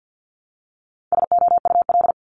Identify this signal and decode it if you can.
Morse code, heard as a keyed tone, about 50 wpm, 700 Hz. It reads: SOUL
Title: Multi-letter Phonemes Repeated 3x 50wpm